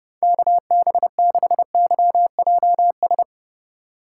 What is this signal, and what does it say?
Morse code, heard as a keyed tone, about 30 wpm, 700 Hz. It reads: KB6YJS
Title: USA Call Signs - 4x Character Spacing - 30wpm